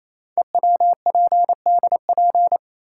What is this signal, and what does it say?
Morse code, heard as a keyed tone, about 27 wpm, 700 Hz. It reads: EWPDP
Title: Code Groups - 5 Letters - 28wpm